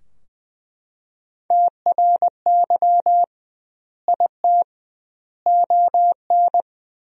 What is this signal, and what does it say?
Morse code, heard as a keyed tone, about 20 wpm, 700 Hz. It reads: TRY IT ON
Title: 2x Word-Spacing - Sentences from Top 400 Words - 20wpm